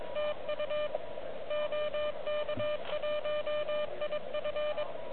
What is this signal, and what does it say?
Morse code, heard as a keyed tone, about 22 wpm, 3000 Hz, with noise. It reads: TU OK1IF
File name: OK QRP 2022 / Zvuk / REC30